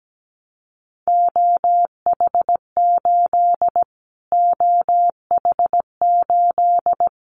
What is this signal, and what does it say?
Morse code, heard as a keyed tone, about 17 wpm, 700 Hz. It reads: OH8 OH8
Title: Callsign Prefixes for Top 25 DXCC - Repeated 3x - 17wpm